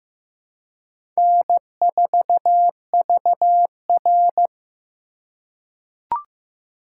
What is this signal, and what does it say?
Morse code, heard as a keyed tone, about 15 wpm, 700 Hz. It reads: N4VR
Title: USA Call Signs - 4 Characters - 4x Character Spacing - 15wpm